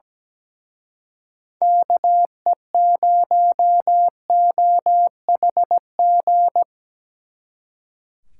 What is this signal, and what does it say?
Morse code, heard as a keyed tone, about 20 wpm, 700 Hz. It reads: KE0OHG
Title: Speed Racing - USA 2x3 Call Signs 17wpm